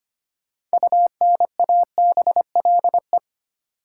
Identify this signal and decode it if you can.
Morse code, heard as a keyed tone, about 25 wpm, 700 Hz. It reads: UNABLE
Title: Six Letter Words - 25wpm